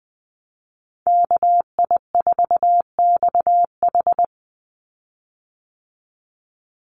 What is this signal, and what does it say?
Morse code, heard as a keyed tone, about 20 wpm, 700 Hz. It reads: KI4XH
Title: Call Signs Repeated 3x - 20wpm